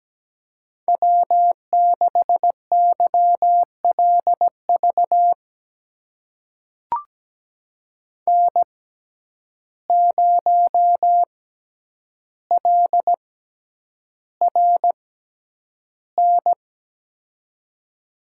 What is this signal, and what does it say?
Morse code, heard as a keyed tone, about 17 wpm, spaced as for 9 wpm, 700 Hz. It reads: W6YLV N0LRN
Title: USA Call Signs - 5 Characters - 6x Character Spacing - 17wpm